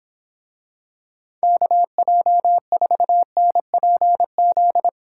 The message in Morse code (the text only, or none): KJ4NPZ